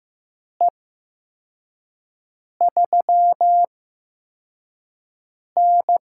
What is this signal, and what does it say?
Morse code, heard as a keyed tone, about 15 wpm, spaced as for 4 wpm, 700 Hz. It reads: E3N
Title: USA Call Signs - 4 Characters - 8x Character Spacing - 15wpm